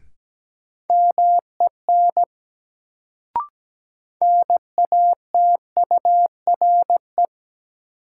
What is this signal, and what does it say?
Morse code, heard as a keyed tone, about 17 wpm, 700 Hz. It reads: MEN NATURE
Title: Sets of 3 Words - New Word Drill - 17wpm